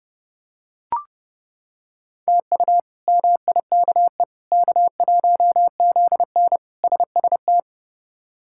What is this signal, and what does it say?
Morse code, heard as a keyed tone, about 30 wpm, 700 Hz. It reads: TU MIKE K1ZN SST
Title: K1USN SST QSO Practice - 30wpm